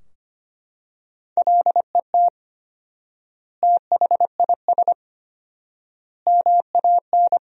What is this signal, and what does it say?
Morse code, heard as a keyed tone, about 25 wpm, 700 Hz. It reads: LET THIS MAN
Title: 4x Word-Spacing - Sentences from Top 400 Words - 25wpm